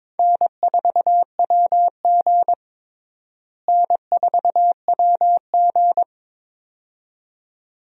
Morse code, heard as a keyed tone, about 22 wpm, 700 Hz. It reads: N4WG N4WG
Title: USA Call Signs - 4 Characters - Repeated 2x - 22wpm